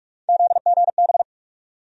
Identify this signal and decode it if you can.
Morse code, heard as a keyed tone, about 42 wpm, 700 Hz. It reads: ZGB
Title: USA Call Sign Postfixes - 3 Letters 45wpm